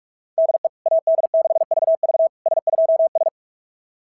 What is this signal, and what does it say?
Morse code, heard as a keyed tone, about 44 wpm, 690 Hz. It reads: DE AD6VV S2S
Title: SOTA QSO Practice 45wpm